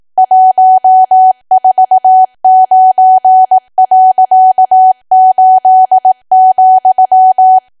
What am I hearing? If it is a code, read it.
Morse code, heard as a keyed tone, about 18 wpm, 750 Hz. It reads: 149.8,
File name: cwa / W1AW-Level2 / 18WPM